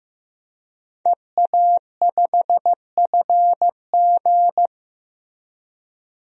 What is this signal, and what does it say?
Morse code, heard as a keyed tone, about 16 wpm, 700 Hz. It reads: EA5FG
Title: DXCC-100 Call Signs - 5 Characters - Repeated 2x - 15wpm